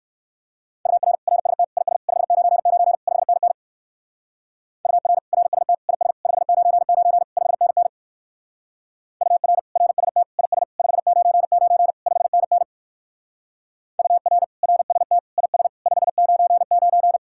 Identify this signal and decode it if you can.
Morse code, heard as a keyed tone, about 66 wpm, 700 Hz. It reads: UR RST IS 599 5NN UR RST IS 599 5NN UR RST IS 599 5NN UR RST IS 599